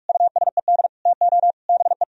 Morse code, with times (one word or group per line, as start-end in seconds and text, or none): 0.1-0.9 USED
1.1-1.5 TO
1.7-2.0 BE